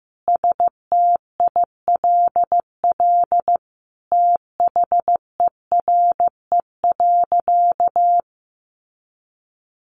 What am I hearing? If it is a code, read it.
Morse code, heard as a keyed tone, about 15 wpm, 700 Hz. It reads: STILL THERE.
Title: Sentences from Top 1500 Words 15wpm